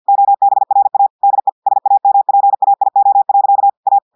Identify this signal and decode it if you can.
Morse code, heard as a keyed tone, about 50 wpm, 800 Hz. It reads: OBRA DE SAMPAIO. A
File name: audios / portugais / portugais texte en clair 1 vitesse 50 mots minutes